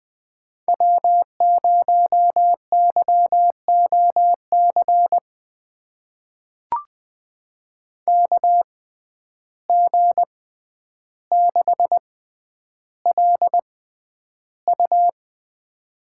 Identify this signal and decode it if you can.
Morse code, heard as a keyed tone, about 20 wpm, spaced as for 10 wpm, 700 Hz. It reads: W0YOC KG6LU
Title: USA Call Signs - 5 Characters - 6x Character Spacing - 20wpm